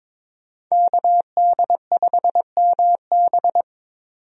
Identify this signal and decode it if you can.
Morse code, heard as a keyed tone, about 22 wpm, 700 Hz. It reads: KD5MB
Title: USA Call Signs - 5 Characters - 22wpm